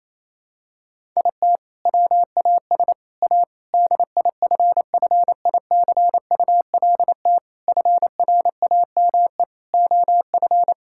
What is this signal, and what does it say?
Morse code, heard as a keyed tone, about 28 wpm, 700 Hz. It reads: IT WAS A DIFFICULT FRAME OF
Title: Sentences from Top 1000 Words - Review Part 6 - 28wpm